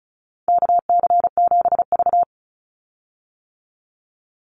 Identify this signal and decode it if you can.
Morse code, heard as a keyed tone, about 34 wpm, 700 Hz. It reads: KC7V
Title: Call Signs - 35wpm